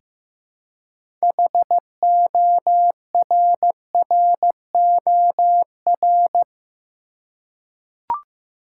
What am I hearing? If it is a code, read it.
Morse code, heard as a keyed tone, about 15 wpm, 700 Hz. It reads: HORROR
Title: Six Letter Words - 15wpm